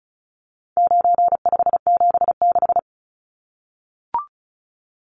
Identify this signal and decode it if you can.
Morse code, heard as a keyed tone, about 34 wpm, 700 Hz. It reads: 9576
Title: Numbers - 4-Digit 35wpm